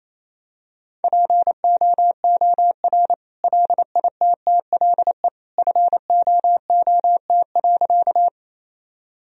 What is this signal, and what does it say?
Morse code, heard as a keyed tone, about 28 wpm, 700 Hz. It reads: POOR LITTLE FOOT.
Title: Sentences from Top 1000 Words - Review Part 3 - 28wpm